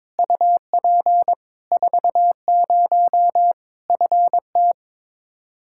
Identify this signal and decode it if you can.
Morse code, heard as a keyed tone, about 22 wpm, 700 Hz. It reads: UP 40 FT
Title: Practice QSO - Exchange 2 - Info - 22wpm